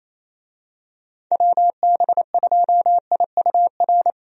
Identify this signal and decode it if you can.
Morse code, heard as a keyed tone, about 28 wpm, 700 Hz. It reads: WB2IUR
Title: USA Call Signs - 6 Characters - 28wpm